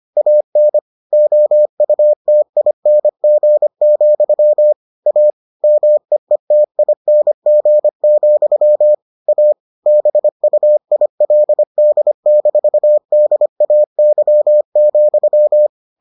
Morse code, heard as a keyed tone, about 25 wpm, 600 Hz. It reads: AN OUTING, A MEETING, A BUILD-DAY,